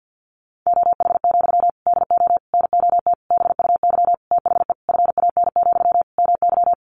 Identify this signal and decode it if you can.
Morse code, heard as a keyed tone, about 50 wpm, 700 Hz. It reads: OH, DO NOT BUY THE FAN, MY
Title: Sentences from Top 1300 Words 50wpm